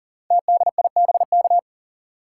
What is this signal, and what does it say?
Morse code, heard as a keyed tone, about 42 wpm, 690 Hz. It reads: TDIBK
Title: Code Groups - 5 Letters - 40wpm